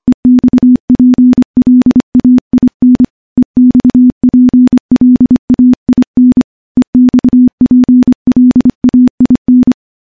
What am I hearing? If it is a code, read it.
Morse code, heard as a keyed tone, about 25 wpm, 250 Hz. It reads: EXPLAIN EXPLAIN EXPLAIN
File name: morse-training-25WPM